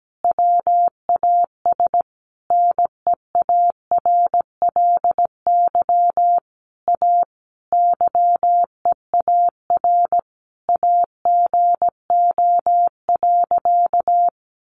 Morse code, heard as a keyed tone, about 17 wpm, 700 Hz. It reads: WAS NEARLY A YEAR AGO.
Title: Sentences from Top 1300 Words 17wpm